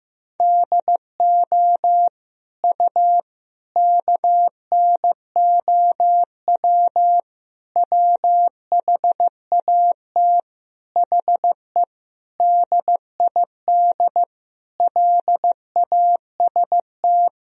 Morse code, heard as a keyed tone, about 16 wpm, 700 Hz. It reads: DO U KNOW WHAT HE DID LAST
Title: Sentences from Top 300 Words and Abbreviations 15wpm